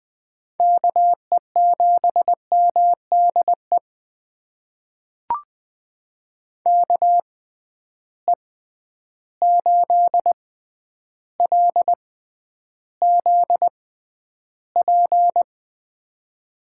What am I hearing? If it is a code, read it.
Morse code, heard as a keyed tone, about 20 wpm, spaced as for 10 wpm, 700 Hz. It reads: KE7MDE KE8LZP
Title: USA Call Signs - 6 Characters - 6x Character Spacing - 20wpm